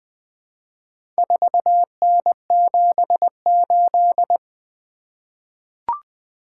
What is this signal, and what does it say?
Morse code, heard as a keyed tone, about 21 wpm, 700 Hz. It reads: 4N78
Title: Four Letter-Number Combinations - 20wpm